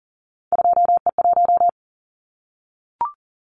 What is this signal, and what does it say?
Morse code, heard as a keyed tone, about 40 wpm, 700 Hz. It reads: 2E1
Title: Callsign Prefixes for Top 25 DXCC - Repeated 3x - 40wpm